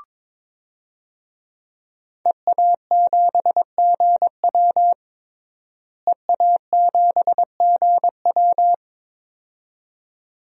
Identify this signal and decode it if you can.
Morse code, heard as a keyed tone, about 22 wpm, 700 Hz. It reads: EA7GW EA7GW
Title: DXCC-100 Call Signs - 5 Characters - Repeated 2x - 22wpm